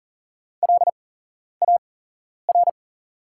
Morse code, heard as a keyed tone, about 40 wpm, spaced as for 11 wpm, 690 Hz. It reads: LAR
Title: 8x Character Spacing - Top 500 Words - 40wpm